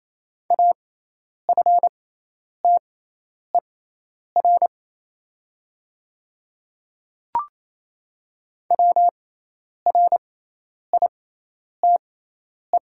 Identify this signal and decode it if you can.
Morse code, heard as a keyed tone, about 28 wpm, spaced as for 10 wpm, 700 Hz. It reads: AFTER WRITE
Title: 6x Character Spacing - Top 200 Words - 28wpm